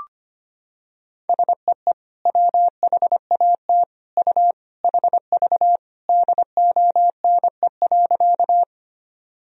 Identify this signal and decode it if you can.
Morse code, heard as a keyed tone, about 25 wpm, 700 Hz. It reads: SEE WHAT U HV DONE.
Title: Sentences from Top 500 Words and Abbreviations 25wpm